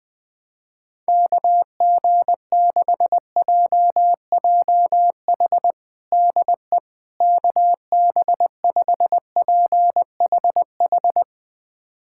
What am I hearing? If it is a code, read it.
Morse code, heard as a keyed tone, about 21 wpm, 700 Hz. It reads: KG6JJH DE KB5PHH
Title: USA Callsign DE Callsign 20wpm